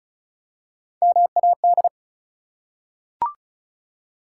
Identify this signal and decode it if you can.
Morse code, heard as a keyed tone, about 36 wpm, 700 Hz. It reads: MAD
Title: Three Letter Words - 35wpm